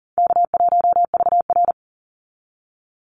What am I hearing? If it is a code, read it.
Morse code, heard as a keyed tone, about 40 wpm, 700 Hz. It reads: K1VR
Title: Easy Call Signs Repeated 3x - 40wpm